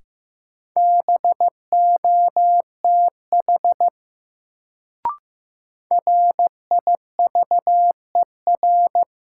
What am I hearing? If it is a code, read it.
Morse code, heard as a keyed tone, about 15 wpm, 700 Hz. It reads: BOTH RIVER
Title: Sentences from Top 300 Words - New Word Drill - 15wpm